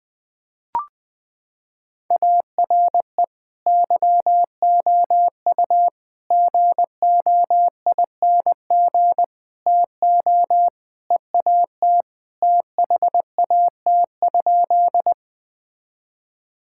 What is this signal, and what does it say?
Morse code, heard as a keyed tone, about 20 wpm, 700 Hz. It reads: ARE YOU GOING TO EAT THAT?
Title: Sentences with 4-Word Phrases - 20wpm